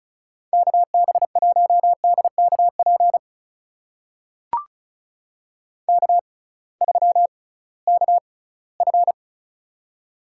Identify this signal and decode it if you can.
Morse code, heard as a keyed tone, about 36 wpm, spaced as for 20 wpm, 700 Hz. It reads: KB1DKP K3KF
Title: USA Call Signs - 6x Character Spacing - 35wpm